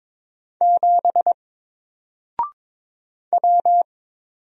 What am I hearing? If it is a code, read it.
Morse code, heard as a keyed tone, about 22 wpm, 700 Hz. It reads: 7 W
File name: Single-Letter-Number-22wpm